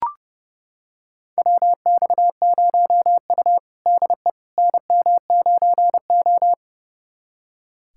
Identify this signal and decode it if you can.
Morse code, heard as a keyed tone, about 29 wpm, 700 Hz. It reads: WX0U DE NM9O
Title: USA Callsign DE Callsign - 4 Characters - 30wpm